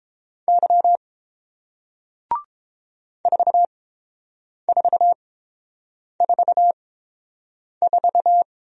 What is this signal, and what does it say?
Morse code, heard as a keyed tone, about 30 wpm, spaced as for 7 wpm, 700 Hz. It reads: Y 4444